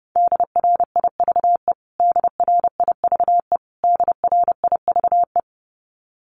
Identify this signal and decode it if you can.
Morse code, heard as a keyed tone, about 28 wpm, 700 Hz. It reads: DRIVE DRIVE DRIVE
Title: Top 500 Words Repeated 30wpm